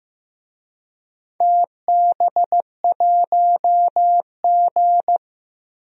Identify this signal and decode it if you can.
Morse code, heard as a keyed tone, about 15 wpm, 700 Hz. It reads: TB1G
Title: DXCC-100 Call Signs - 4 Characters - 15wpm